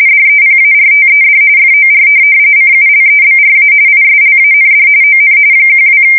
RTTY, about 45.45 baud. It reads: DEPARTED FOR CULEBRA ISLAND ON 12
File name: fsk170 / USS-Missouri